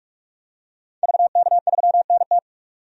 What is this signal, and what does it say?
Morse code, heard as a keyed tone, about 46 wpm, 700 Hz. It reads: VK3NT
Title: DXCC-100 Call Signs - 5 Characters - 45wpm